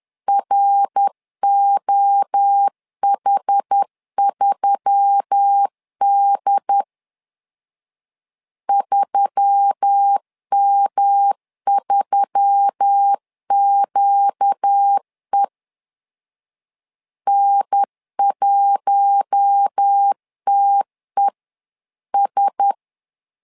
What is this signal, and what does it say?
Morse code, heard as a keyed tone, about 10 wpm, 800 Hz. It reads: ROH3D 3M3QE N1TES
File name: LEZIONI / 4 Q1S3 10wpm 2427